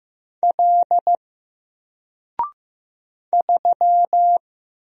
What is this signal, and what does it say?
Morse code, heard as a keyed tone, about 15 wpm, 700 Hz. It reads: L 3
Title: Single Character (Letter:Number:Punctuation:Prosign) Alt 15wpm